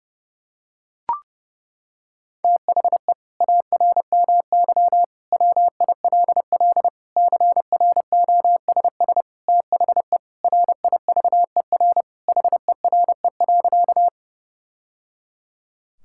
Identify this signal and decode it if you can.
Morse code, heard as a keyed tone, about 32 wpm, 700 Hz. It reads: THE ARMY WILL CROSS THE RIVER HERE.
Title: Sentences from Top 1000 Words - Review Part 5 - 30wpm